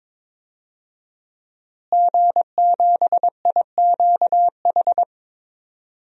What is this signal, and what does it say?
Morse code, heard as a keyed tone, about 22 wpm, 700 Hz. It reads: G7IQH
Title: DXCC-100 Call Signs - 5 Characters - Repeated 2x - 22wpm